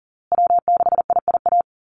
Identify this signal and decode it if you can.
Morse code, heard as a keyed tone, about 40 wpm, 700 Hz. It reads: W6IIA
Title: USA Call Signs Repeated 2x 40wpm